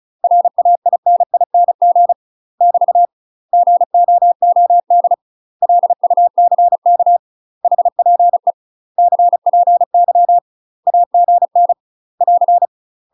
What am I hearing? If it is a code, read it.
Morse code, heard as a keyed tone, about 34 wpm, 700 Hz. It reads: RAINING <BT> GOOD LUCK HPE CPY AGN <AR>
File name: UK MASTER QSOs TRAINING / 35WPM / UK MASTER QSOs 0004